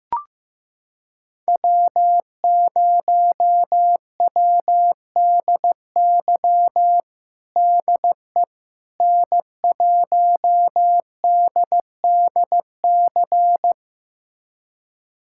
Morse code, heard as a keyed tone, about 15 wpm, 700 Hz. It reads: W0WDY DE N1DDC
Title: USA Callsign DE Callsign - 5 Characters - 15wpm